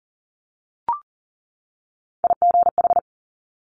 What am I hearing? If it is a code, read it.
Morse code, heard as a keyed tone, about 40 wpm, 700 Hz. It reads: IGH
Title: Top Three Letter Combinations in Words 40wpm